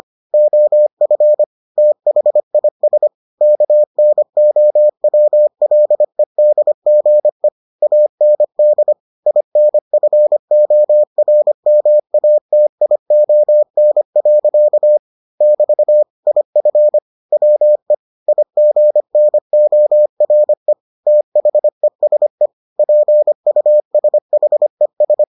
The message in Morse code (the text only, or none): OF THIS KNOWLEDGE AND INFORMATION. <BT> IF WE IGNORE THESE PUSHES